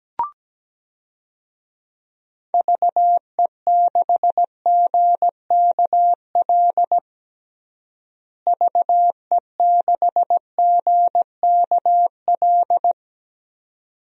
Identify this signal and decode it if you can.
Morse code, heard as a keyed tone, about 17 wpm, 700 Hz. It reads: VE6GKL VE6GKL
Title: DXCC-100 Call Signs - 6 Characters - Repeated 2x - 17wpm